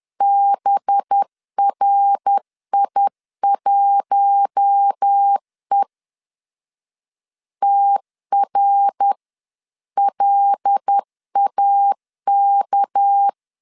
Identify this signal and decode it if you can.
Morse code, heard as a keyed tone, about 10 wpm, 800 Hz. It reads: BRI1E TRLAK
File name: LEZIONI / 6 PKX0 / 6 PK 10wpm 2436